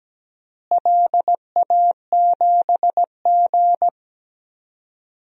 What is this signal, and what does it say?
Morse code, heard as a keyed tone, about 17 wpm, 700 Hz. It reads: LA7G
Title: DXCC-100 Call Signs - 4 Characters - Repeated 2x - 17wpm